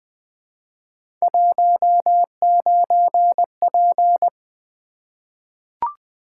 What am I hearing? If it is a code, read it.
Morse code, heard as a keyed tone, about 20 wpm, 700 Hz. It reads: 19P
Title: Three Letter-Number Combinations - 20wpm